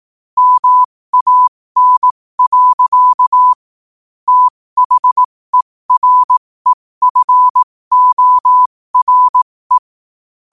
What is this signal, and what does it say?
Morse code, heard as a keyed tone, about 18 wpm, spaced as for 15 wpm, 1000 Hz. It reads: MAN. THEREFORE
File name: qotdfiles / 15wpm